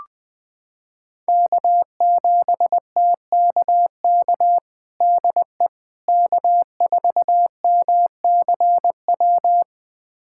Morse code, heard as a keyed tone, about 21 wpm, 700 Hz. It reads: K7TKK DE K4MCW
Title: USA Callsign DE Callsign - 5 Characters - 20wpm